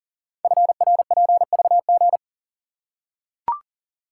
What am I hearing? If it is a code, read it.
Morse code, heard as a keyed tone, about 40 wpm, 700 Hz. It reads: FRPVG
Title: Code Groups - 5 Letters - 40wpm